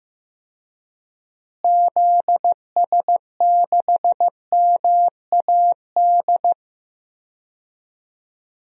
Morse code, heard as a keyed tone, about 15 wpm, 700 Hz. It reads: ZS6MAD
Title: DXCC-100 Call Signs - 6 Characters - Repeated 2x - 15wpm